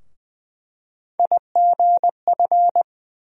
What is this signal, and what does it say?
Morse code, heard as a keyed tone, about 20 wpm, 700 Hz. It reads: IGF
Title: USA Call Sign Postfixes - 3 Letters 20wpm